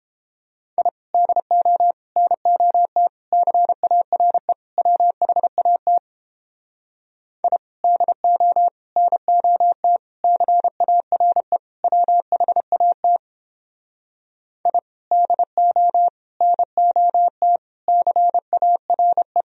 Morse code, heard as a keyed tone, about 31 wpm, 700 Hz. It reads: I DO NOT CARE WHAT I DO NOT CARE WHAT I DO NOT CARE